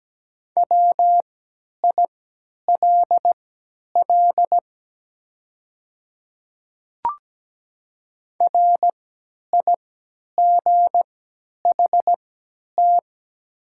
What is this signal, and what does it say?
Morse code, heard as a keyed tone, about 17 wpm, spaced as for 10 wpm, 700 Hz. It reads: WILL RIGHT
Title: 3x Character Spacing - Top 200 Words - 17wpm